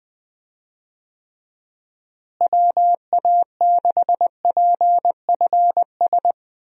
Morse code, heard as a keyed tone, about 21 wpm, 700 Hz. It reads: WA6PFS
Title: USA Call Signs - 6 Characters - Repeated 2x - 20wpm